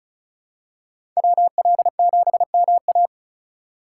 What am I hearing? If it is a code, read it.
Morse code, heard as a keyed tone, about 36 wpm, 700 Hz. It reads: WL7MA
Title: USA Call Signs - 5 Characters - 2x Character Spacing - 35wpm